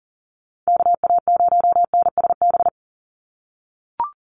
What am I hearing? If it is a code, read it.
Morse code, heard as a keyed tone, about 40 wpm, 700 Hz. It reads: KA0NSB
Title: USA Call Signs 40wpm